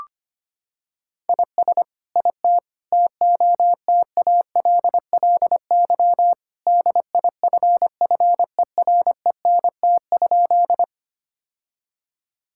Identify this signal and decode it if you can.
Morse code, heard as a keyed tone, about 25 wpm, 700 Hz. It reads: IS IT TOTALLY DIFFERENT?
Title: Sentences from Top 1600 Words 25wpm